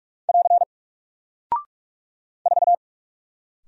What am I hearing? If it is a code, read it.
Morse code, heard as a keyed tone, about 44 wpm, 700 Hz. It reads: <AR> 4
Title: Single Character (Letter:Number:Punctuation:Prosign) Alt 45wpm